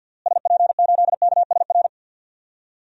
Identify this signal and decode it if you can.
Morse code, heard as a keyed tone, about 51 wpm, 700 Hz. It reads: SP8KSR
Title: DXCC-100 Call Signs - 6 Characters - Repeated 3x - 50wpm